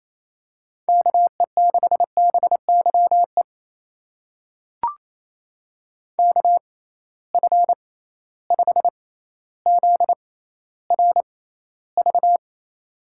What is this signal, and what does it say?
Morse code, heard as a keyed tone, about 28 wpm, spaced as for 14 wpm, 700 Hz. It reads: KE6BYE KF5ZRV
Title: USA Call Signs - 6 Characters - 6x Character Spacing - 28wpm